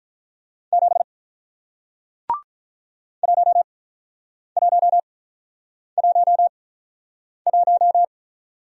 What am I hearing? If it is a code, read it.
Morse code, heard as a keyed tone, about 47 wpm, spaced as for 9 wpm, 700 Hz. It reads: 7 1111